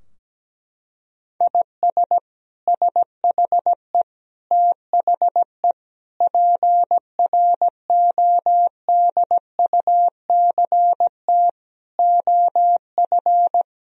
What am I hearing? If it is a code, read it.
Morse code, heard as a keyed tone, about 17 wpm, 700 Hz. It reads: IS SHE THE PRODUCT OF
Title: Sentences from Top 1000 Words - Review Part 2 - 17wpm